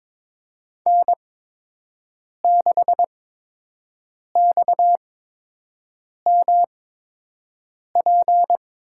Morse code, heard as a keyed tone, about 22 wpm, spaced as for 6 wpm, 700 Hz. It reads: N6XMP
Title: USA Call Signs - 8x Character Spacing - 22wpm